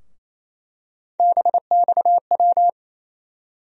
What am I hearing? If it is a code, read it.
Morse code, heard as a keyed tone, about 28 wpm, 700 Hz. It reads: BXW
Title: USA Call Sign Postfixes - 3 Letters 28wpm